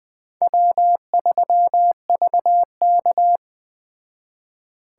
Morse code, heard as a keyed tone, about 20 wpm, 700 Hz. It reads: W3VK